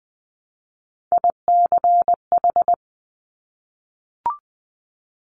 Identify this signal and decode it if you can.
Morse code, heard as a keyed tone, about 19 wpm, 700 Hz. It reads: ICH
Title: Top Three Letter Combinations in Words 20wpm